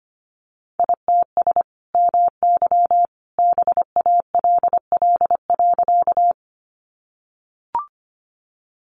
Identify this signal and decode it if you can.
Morse code, heard as a keyed tone, about 25 wpm, 700 Hz. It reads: ITS MY BALL.
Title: Sentences from Top 900 Words 25wpm